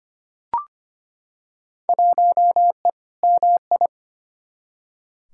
Morse code, heard as a keyed tone, about 25 wpm, 700 Hz. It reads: 1E MI
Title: Field Day Exchanges - 25wpm